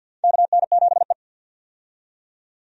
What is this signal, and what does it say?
Morse code, heard as a keyed tone, about 46 wpm, 700 Hz. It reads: KN7E
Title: USA Call Signs - 4 Characters - 50wpm